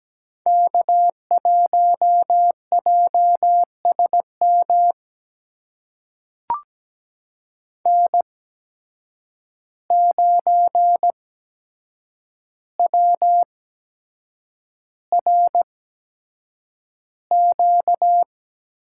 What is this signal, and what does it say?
Morse code, heard as a keyed tone, about 17 wpm, spaced as for 7 wpm, 700 Hz. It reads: K1JSM N9WRQ
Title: USA Call Signs - 5 Characters - 8x Character Spacing - 17wpm